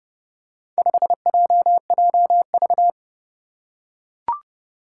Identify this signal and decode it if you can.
Morse code, heard as a keyed tone, about 30 wpm, 700 Hz. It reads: 5JJV
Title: Four Challenging Letter-Number Combinations 30wpm